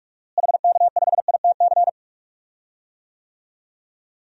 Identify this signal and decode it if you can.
Morse code, heard as a keyed tone, about 41 wpm, 700 Hz. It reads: HK5ITC
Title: DXCC-100 Call Signs - 6 Characters - Repeated 2x - 45wpm